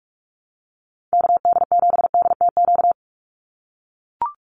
Morse code, heard as a keyed tone, about 44 wpm, 700 Hz. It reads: KD7DTQ
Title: USA Call Signs 45wpm